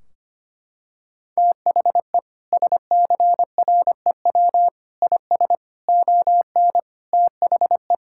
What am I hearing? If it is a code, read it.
Morse code, heard as a keyed tone, about 25 wpm, 700 Hz. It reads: THE SCREW IS ON THE
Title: Sentences from Top 1600 Words 25wpm